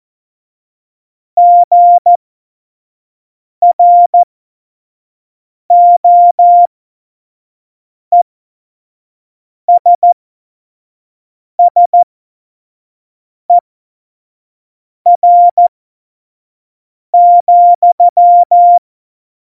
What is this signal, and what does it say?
Morse code, heard as a keyed tone, about 14 wpm, spaced as for 5 wpm, 700 Hz. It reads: GROESSER,
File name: 25 F70